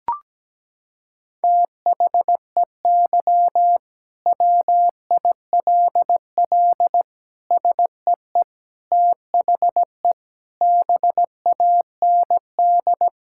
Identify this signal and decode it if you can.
Morse code, heard as a keyed tone, about 17 wpm, 700 Hz. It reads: THEY WILL SEE THE BAND
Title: Sentences from Top 1600 Words 17wpm